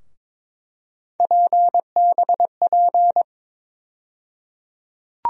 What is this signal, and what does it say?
Morse code, heard as a keyed tone, about 22 wpm, 700 Hz. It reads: PBP
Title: USA Call Sign Postfixes - 3 Letters 22wpm